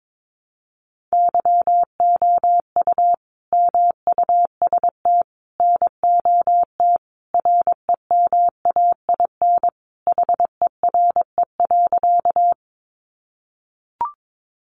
Morse code, heard as a keyed tone, about 22 wpm, 700 Hz. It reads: YOU MUST NOT REMAIN HERE.
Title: Sentences from Top 1200 Words 22wpm